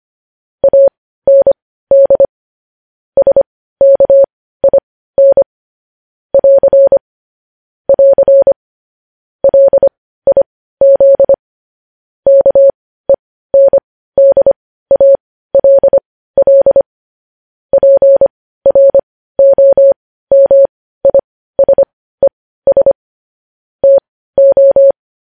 Morse code, heard as a keyed tone, about 25 wpm, spaced as for 15 wpm, 550 Hz. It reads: AND SKIN <AR> <AR> LIZ KENDALL PROMISES TO
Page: news-headlines-in-morse-code-at-15-wpm